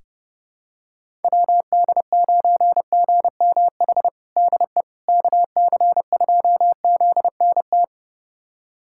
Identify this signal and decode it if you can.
Morse code, heard as a keyed tone, about 32 wpm, 700 Hz. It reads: WD9GMH DE KC2ZNT
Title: USA Callsign DE Callsign 30wpm